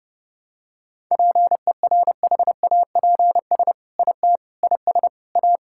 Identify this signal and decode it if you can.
Morse code, heard as a keyed tone, about 30 wpm, 700 Hz. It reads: PERHAPS IT IS A
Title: Sentences from Top 1000 Words - Review Part 5 - 30wpm